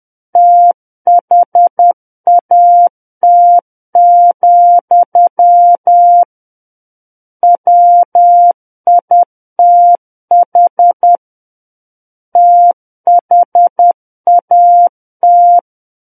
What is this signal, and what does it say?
Morse code, heard as a keyed tone, about 10 wpm, 700 Hz. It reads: THAT, WITH THAT
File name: Book One / 01-Chapter 1 - 10 WPM